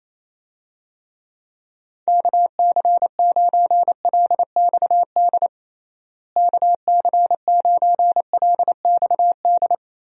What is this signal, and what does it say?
Morse code, heard as a keyed tone, about 28 wpm, 700 Hz. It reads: KC9LXD KC9LXD
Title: USA Call Signs - 6 Characters - Repeated 3x - 28wpm